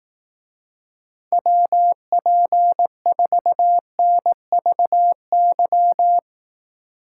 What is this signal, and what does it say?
Morse code, heard as a keyed tone, about 18 wpm, 700 Hz. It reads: WP4NVY
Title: Speed Racing - USA 2x3 Call Signs 15wpm